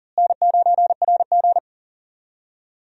Morse code, heard as a keyed tone, about 40 wpm, 700 Hz. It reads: N9RG
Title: USA Call Signs - 4 Characters - Repeated 3x - 40wpm